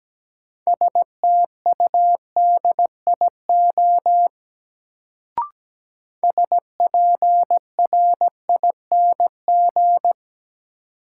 Morse code, heard as a keyed tone, about 17 wpm, 700 Hz. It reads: STUDIO SPRING
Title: Sentences from Top 1200 Words - New Word Drill - 17wpm